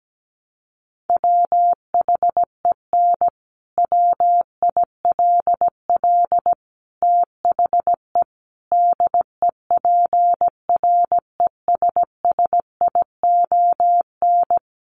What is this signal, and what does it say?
Morse code, heard as a keyed tone, about 17 wpm, 700 Hz. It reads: WHEN WILL THE DEPRESSION
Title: Sentences from Top 1300 Words 17wpm